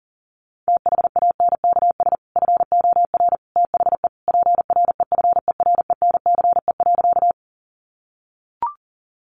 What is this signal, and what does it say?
Morse code, heard as a keyed tone, about 40 wpm, 700 Hz. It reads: THANKS FOR THE PREFERENCE.
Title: Sentences from Top 1500 Words 40wpm